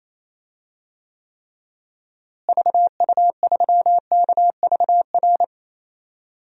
Morse code, heard as a keyed tone, about 27 wpm, 700 Hz. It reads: VU3KVR
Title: DXCC-100 Call Signs - 6 Characters - Repeated 3x - 28wpm